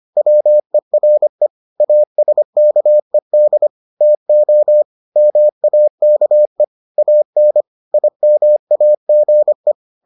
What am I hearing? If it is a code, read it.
Morse code, heard as a keyed tone, about 25 wpm, 600 Hz. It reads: WERE ASKED TO MAKE AN IMAGE